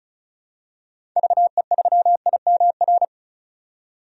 Morse code, heard as a keyed tone, about 36 wpm, 700 Hz. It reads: VE3IMR
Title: DXCC-100 Call Signs - 6 Characters - 35wpm